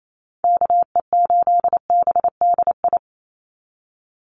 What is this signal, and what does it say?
Morse code, heard as a keyed tone, about 28 wpm, 700 Hz. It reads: KE8BDI
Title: USA Call Signs 28wpm